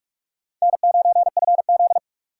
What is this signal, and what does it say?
Morse code, heard as a keyed tone, about 60 wpm, spaced as for 52 wpm, 700 Hz. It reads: N9FZ
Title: Speed Racing - USA 1x2 Call Signs 45wpm